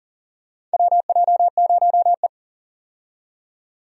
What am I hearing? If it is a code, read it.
Morse code, heard as a keyed tone, about 40 wpm, 700 Hz. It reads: WJ0E
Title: USA Call Signs - 4 Characters - 40wpm